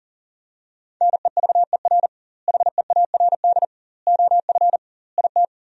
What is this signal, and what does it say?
Morse code, heard as a keyed tone, about 40 wpm, 700 Hz. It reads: NEVER HEARD OF IT
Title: 2x Word-Spacing - Sentences from Top 400 Words - 40wpm